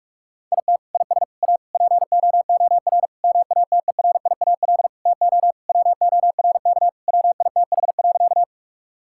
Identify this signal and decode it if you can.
Morse code, heard as a keyed tone, about 46 wpm, 700 Hz. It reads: IT IS A POOR MATERIAL TO WORK WITH.
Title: Sentences from Top 1000 Words - Review Part 5 - 45wpm